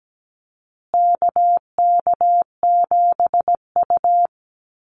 Morse code, heard as a keyed tone, about 17 wpm, 700 Hz. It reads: KK7U
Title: USA Call Signs Repeated 2x 17wpm